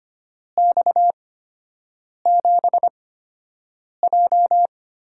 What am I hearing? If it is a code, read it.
Morse code, heard as a keyed tone, about 25 wpm, spaced as for 7 wpm, 700 Hz. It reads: X7J
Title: USA Call Signs - 8x Character Spacing - 25wpm